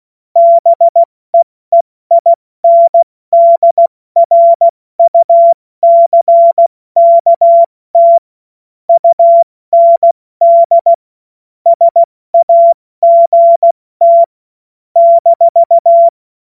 Morse code, heard as a keyed tone, about 16 wpm, 680 Hz. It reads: BEEINDRUCKT UND SAGT -
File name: Morsekurs / Witze / Witz 09. Februar 2021, 70 F80